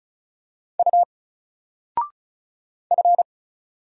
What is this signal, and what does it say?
Morse code, heard as a keyed tone, about 35 wpm, 700 Hz. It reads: U F
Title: Single Letter-Number 35wpm